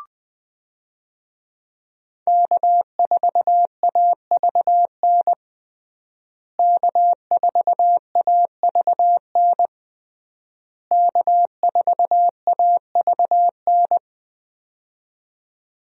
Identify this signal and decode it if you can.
Morse code, heard as a keyed tone, about 20 wpm, 700 Hz. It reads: K4AVN K4AVN K4AVN
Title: USA Call Signs - 5 Characters - Repeated 3x - 20wpm